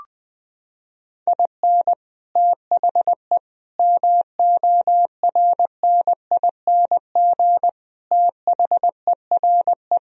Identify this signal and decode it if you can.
Morse code, heard as a keyed tone, about 21 wpm, 700 Hz. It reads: IN THE MORNING THERE
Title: Sentences from Top 1000 Words - Review Part 1 - 20wpm